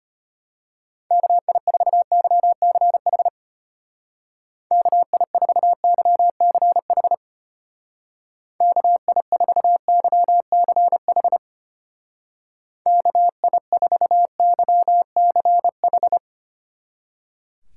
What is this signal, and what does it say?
Morse code, heard as a keyed tone, about 31 wpm, 700 Hz. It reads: KI4YCH KI4YCH KI4YCH KI4YCH
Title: Speed Racing - USA 2x3 Call Signs 25wpm